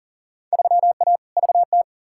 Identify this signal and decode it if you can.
Morse code, heard as a keyed tone, about 37 wpm, 700 Hz. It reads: 3A VT
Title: Field Day Exchanges Encore - 40wpm